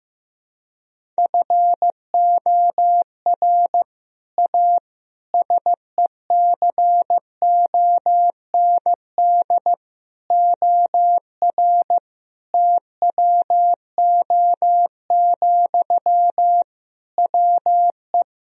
Morse code, heard as a keyed tone, about 15 wpm, 700 Hz. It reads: FOR A SECOND OR TWO, WE
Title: Sentences from Top 1000 Words - Review Part 1 - 15wpm